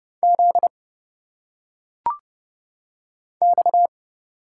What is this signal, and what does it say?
Morse code, heard as a keyed tone, about 30 wpm, 700 Hz. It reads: Z X
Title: Single Letters - Phonetic Pronunciation 30wpm